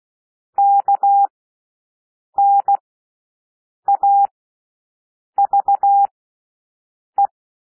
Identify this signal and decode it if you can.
Morse code, heard as a keyed tone, about 16 wpm, spaced as for 6 wpm, 800 Hz. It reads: KNAVE
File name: cwmp3 / W1 SOUND WORDS 16 wpm